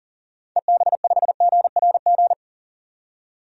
Morse code, heard as a keyed tone, about 40 wpm, 700 Hz. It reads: EB5GRG
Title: DXCC-100 Call Signs - 6 Characters - 40wpm